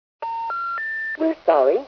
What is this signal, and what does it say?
Special information tone, with speech.